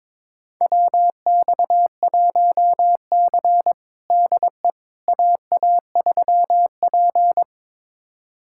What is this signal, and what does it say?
Morse code, heard as a keyed tone, about 22 wpm, 700 Hz. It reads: WX1C DE AA3P
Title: USA Callsign DE Callsign - 4 Characters - 22wpm